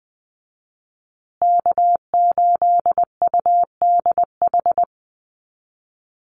Morse code, heard as a keyed tone, about 20 wpm, 700 Hz. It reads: K8UDH
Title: Call Signs Repeated 3x - 20wpm